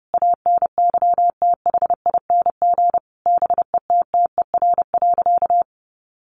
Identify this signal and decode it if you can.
Morse code, heard as a keyed tone, about 30 wpm, 700 Hz. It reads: ANYTHING BETTER.
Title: Sentences from Top 1100 Words 30wpm